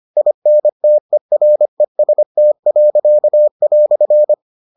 Morse code, heard as a keyed tone, about 25 wpm, 600 Hz. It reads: INTEREST."